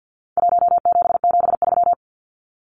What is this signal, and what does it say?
Morse code, heard as a keyed tone, about 50 wpm, 700 Hz. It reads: 1773
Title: Numbers - Contest Style - 4-Digit 50wpm